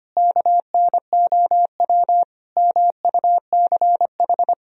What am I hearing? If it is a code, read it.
Morse code, heard as a keyed tone, about 25 wpm, 700 Hz. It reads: KNOW MUCH
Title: Sentences from Top 1000 Words - Review Part 2 - 25wpm